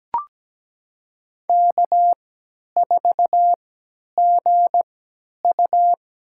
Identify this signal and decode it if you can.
Morse code, heard as a keyed tone, about 17 wpm, spaced as for 10 wpm, 700 Hz. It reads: K4GU
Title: USA Call Signs - 4 Characters - 3x Character Spacing - 17wpm